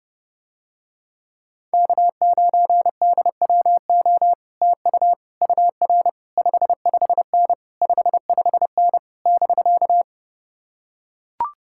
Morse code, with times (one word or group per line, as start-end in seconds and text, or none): 1.7-4.3 K9DWO
4.6-5.1 TU
5.4-6.1 UR
6.4-7.5 55N
7.8-9.0 55N
9.3-10.0 <BK>